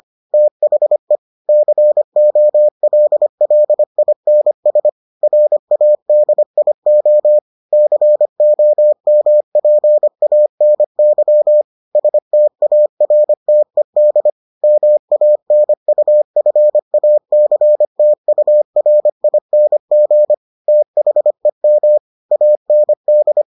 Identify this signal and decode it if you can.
Morse code, heard as a keyed tone, about 25 wpm, 600 Hz. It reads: THE COLLINS RADIO COMPANY STARTED MANUFACTURING THEM AND